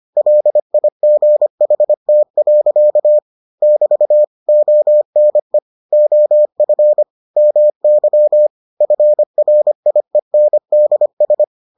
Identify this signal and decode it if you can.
Morse code, heard as a keyed tone, about 25 wpm, 600 Hz. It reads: LIGHT. <BT> ONE OF MY FRIENDS